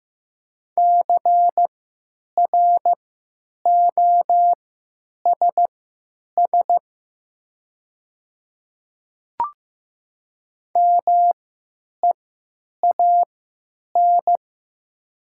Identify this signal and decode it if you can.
Morse code, heard as a keyed tone, about 15 wpm, spaced as for 9 wpm, 700 Hz. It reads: CROSS MEAN
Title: 3x Character Spacing - Top 300 Words - 15wpm